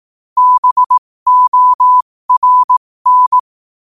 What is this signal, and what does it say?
Morse code, heard as a keyed tone, about 18 wpm, spaced as for 15 wpm, 1000 Hz. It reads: BORN
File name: qotdfiles / 15wpm